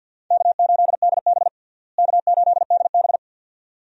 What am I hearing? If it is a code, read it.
Morse code, heard as a keyed tone, about 52 wpm, 700 Hz. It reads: K8DB K8DB
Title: USA Call Signs - 4 Characters - Repeated 2x - 50wpm